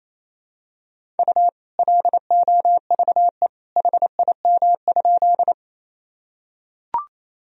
Morse code, heard as a keyed tone, about 28 wpm, 700 Hz. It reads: U LOVE HIM?
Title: Sentences from Top 400 Words and Abbreviations 28wpm